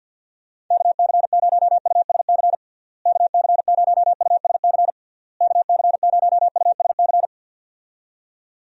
Morse code, heard as a keyed tone, about 51 wpm, 700 Hz. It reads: KC0USC KC0USC KC0USC
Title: USA Call Signs - 6 Characters - Repeated 3x - 50wpm